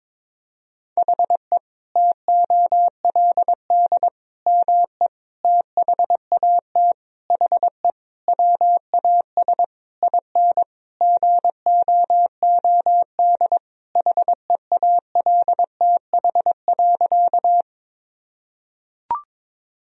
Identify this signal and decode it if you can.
Morse code, heard as a keyed tone, about 22 wpm, 700 Hz. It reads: HE TOLD ME THAT HE WAS IN GOOD HEALTH.
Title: Sentences from Top 1000 Words - Review Part 3 - 22wpm